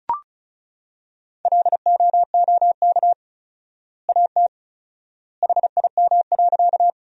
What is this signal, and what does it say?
Morse code, heard as a keyed tone, about 36 wpm, 700 Hz. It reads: LOOK AT HIM.
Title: 4x Word-Spacing - Sentences from Top 100 Words - 35wpm